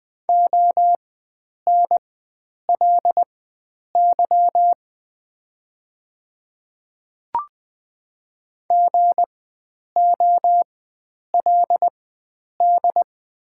Morse code, heard as a keyed tone, about 20 wpm, spaced as for 9 wpm, 690 Hz. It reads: ONLY GOLD
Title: 4x Character Spacing - Top 500 Words - 20wpm